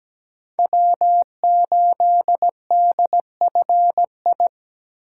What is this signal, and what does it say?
Morse code, heard as a keyed tone, about 17 wpm, 700 Hz. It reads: W8DFI
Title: USA Call Signs - 8x Character Spacing - 17wpm